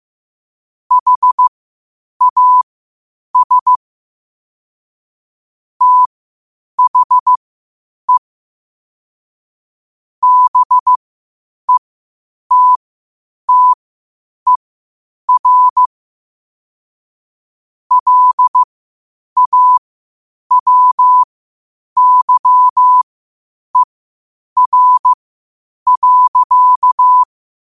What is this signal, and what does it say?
Morse code, heard as a keyed tone, about 15 wpm, spaced as for 8 wpm, 1000 Hz. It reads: HAS THE BETTER LAWYER.
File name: qotdfiles / 8wpm